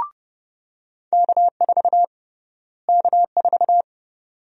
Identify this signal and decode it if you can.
Morse code, heard as a keyed tone, about 30 wpm, 700 Hz. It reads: K4 K4
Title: USA Call Sign Prefixes - Repeated 2x - 30wpm